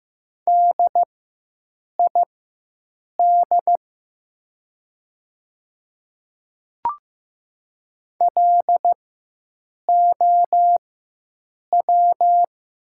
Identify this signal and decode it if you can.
Morse code, heard as a keyed tone, about 15 wpm, spaced as for 7 wpm, 700 Hz. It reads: DID LOW
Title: 4x Character Spacing - Top 200 Words - 15wpm